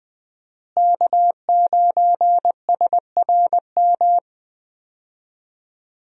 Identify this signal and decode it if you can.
Morse code, heard as a keyed tone, about 20 wpm, 700 Hz. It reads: K9SRM
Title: USA Call Signs - 5 Characters - Repeated 2x - 20wpm